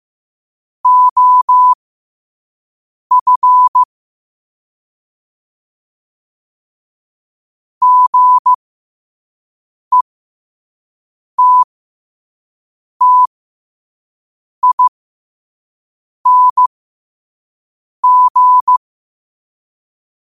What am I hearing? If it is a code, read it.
Morse code, heard as a keyed tone, about 15 wpm, spaced as for 5 wpm, 1000 Hz. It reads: OF GETTING
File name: qotdfiles / 5wpm